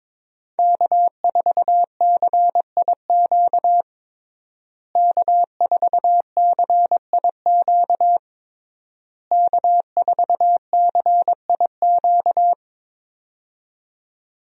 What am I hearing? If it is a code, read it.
Morse code, heard as a keyed tone, about 22 wpm, 700 Hz. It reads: K4CIQ K4CIQ K4CIQ
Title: USA Call Signs - 5 Characters - Repeated 3x - 22wpm